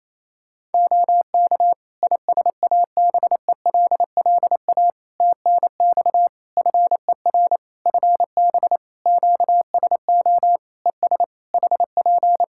Morse code, heard as a keyed tone, about 28 wpm, 700 Hz. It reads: OK ISABELLA TNX FER FB QSO ES HP
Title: Practice QSO - Exchange 4 - Ending - 28wpm